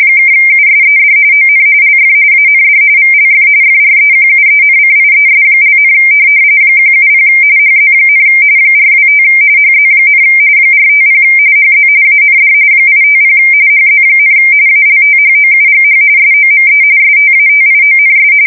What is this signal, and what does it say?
RTTY, about 45.45 baud. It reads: (USED ON LF AND VLF FREQUENCIES), 170 HZ, 425 HZ, 450 HZ AND 850 HZ, ALTHOUGH SOME STATIONS
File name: fsk170 / teletype-RTTY